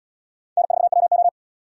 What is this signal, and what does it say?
Morse code, heard as a keyed tone, about 70 wpm, 700 Hz. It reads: N4WO